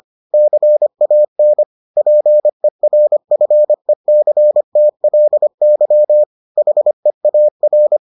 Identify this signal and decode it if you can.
Morse code, heard as a keyed tone, about 25 wpm, 600 Hz. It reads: CAN PERFECTLY HEAR